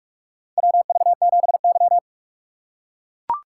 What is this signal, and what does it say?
Morse code, heard as a keyed tone, about 42 wpm, 700 Hz. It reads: WV7Y
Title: USA Call Signs - 4 Characters - 2x Character Spacing - 45wpm